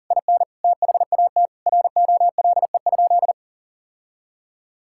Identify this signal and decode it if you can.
Morse code, heard as a keyed tone, about 40 wpm, 700 Hz. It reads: IN THAT ROLE?
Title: Sentences from Top 1000 Words - Review Part 6 - 40wpm